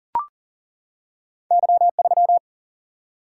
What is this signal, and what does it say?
Morse code, heard as a keyed tone, about 40 wpm, 700 Hz. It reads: Y3
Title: Single-Letter Single-Number 40wpm